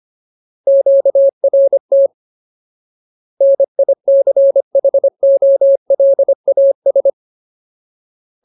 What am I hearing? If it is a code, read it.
Morse code, heard as a keyed tone, about 25 wpm, 550 Hz. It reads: QRT NICHOLAS
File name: Ultimate Word List / 25WPM Ultimate